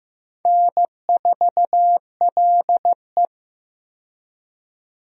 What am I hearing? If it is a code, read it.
Morse code, heard as a keyed tone, about 15 wpm, 700 Hz. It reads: N4LE
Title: USA Call Signs - 4 Characters - 15wpm